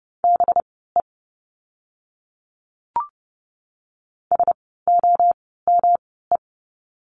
Morse code, heard as a keyed tone, about 30 wpm, spaced as for 17 wpm, 700 Hz. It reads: BE SOME
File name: course / 30:10wpm / 237 Lesson 38f - Letter q - Words 30wpm